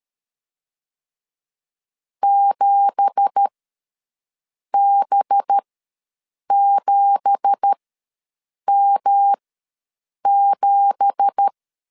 Morse code, heard as a keyed tone, about 13 wpm, spaced as for 6 wpm, 800 Hz. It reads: 7B7M7
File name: LEZIONI / 9 2867 ws 12wpm 3431